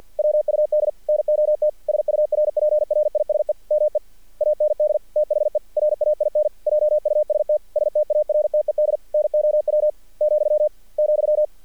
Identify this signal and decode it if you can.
Morse code, heard as a keyed tone, about 48 wpm, 600 Hz. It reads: WUD NOT SURPRISE ME AND THE RAIN JUST STARTED NOW , ,
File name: archived / CRSnet